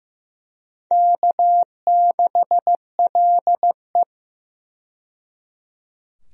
Morse code, heard as a keyed tone, about 15 wpm, 700 Hz. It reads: K6LE
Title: USA Call Signs - 4 Characters - 15wpm